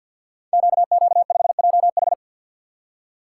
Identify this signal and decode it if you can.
Morse code, heard as a keyed tone, about 52 wpm, 690 Hz. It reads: QQ5JH
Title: Code Groups - 5 Letter:Numbers - 50wpm